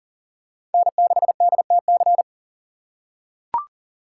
Morse code, heard as a keyed tone, about 40 wpm, 700 Hz. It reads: N6DTC
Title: USA Call Signs - 5 Characters - 3x Character Spacing - 40wpm